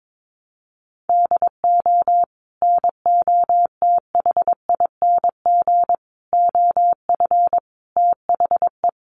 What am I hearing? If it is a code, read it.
Morse code, heard as a keyed tone, about 22 wpm, 700 Hz. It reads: DO NOTHING OF THE
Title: Sentences from Top 1300 Words 22wpm